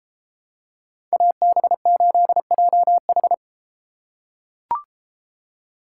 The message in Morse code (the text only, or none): AB8JH